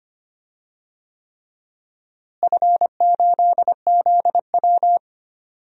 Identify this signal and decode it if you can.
Morse code, heard as a keyed tone, about 25 wpm, 700 Hz. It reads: F8ZW
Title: DXCC-100 Call Signs - 4 Characters - Repeated 3x - 25wpm